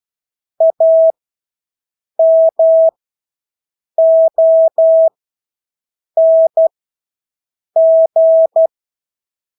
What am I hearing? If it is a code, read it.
Morse code, heard as a keyed tone, about 12 wpm, spaced as for 6 wpm, 650 Hz. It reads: AMONG
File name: Engelske ord 12-6 wpm